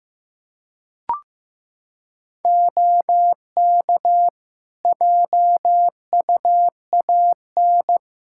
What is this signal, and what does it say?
Morse code, heard as a keyed tone, about 15 wpm, 700 Hz. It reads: OK JUAN
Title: Practice QSO - Exchange 3 - More Info - 15wpm